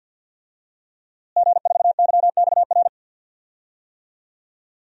Morse code, heard as a keyed tone, about 52 wpm, 700 Hz. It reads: G4YXR
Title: DXCC-100 Call Signs - 5 Characters - Repeated 2x - 50wpm